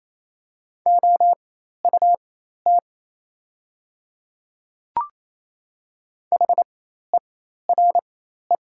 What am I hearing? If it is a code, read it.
Morse code, heard as a keyed tone, about 28 wpm, spaced as for 13 wpm, 700 Hz. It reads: OUT HERE
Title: 4x Character Spacing - Top 300 Words - 28wpm